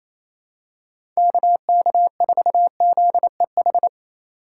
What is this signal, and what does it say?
Morse code, heard as a keyed tone, about 28 wpm, 700 Hz. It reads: KK4ZEH
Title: USA Call Signs - 6 Characters - 3x Character Spacing - 28wpm